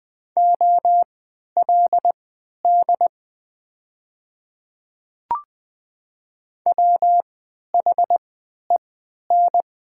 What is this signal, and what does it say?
Morse code, heard as a keyed tone, about 20 wpm, spaced as for 11 wpm, 700 Hz. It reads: OLD WHEN
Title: 3x Character Spacing - Top 200 Words - 20wpm